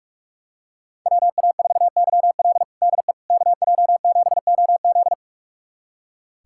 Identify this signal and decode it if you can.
Morse code, heard as a keyed tone, about 46 wpm, 700 Hz. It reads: WA4YL DE KJ7OZ
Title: USA Callsign DE Callsign 45wpm